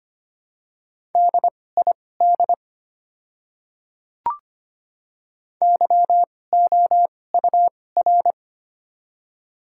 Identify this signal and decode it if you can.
Morse code, heard as a keyed tone, about 25 wpm, spaced as for 18 wpm, 690 Hz. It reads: DID YOUR
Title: 2x Character Spacing - Top 100 Words - 25wpm